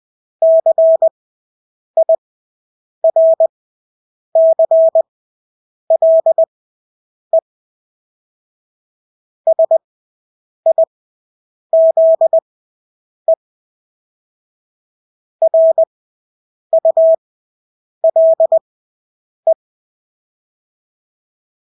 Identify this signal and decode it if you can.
Morse code, heard as a keyed tone, about 20 wpm, spaced as for 8 wpm, 650 Hz. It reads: CIRCLE SIZE RULE
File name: Engelske ord 20/8 wpm